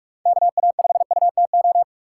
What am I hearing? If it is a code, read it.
Morse code, heard as a keyed tone, about 42 wpm, 700 Hz. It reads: KA5UTO
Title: USA Call Signs - 6 Characters - 4x Character Spacing - 45wpm